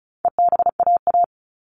Morse code, heard as a keyed tone, about 34 wpm, 700 Hz. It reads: EBAA
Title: All Cut Numbers - 4-Digit 35wpm